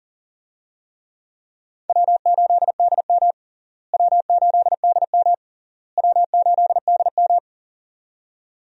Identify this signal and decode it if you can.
Morse code, heard as a keyed tone, about 40 wpm, 700 Hz. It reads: W8DM W8DM W8DM
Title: USA Call Signs - 4 Characters - Repeated 3x - 40wpm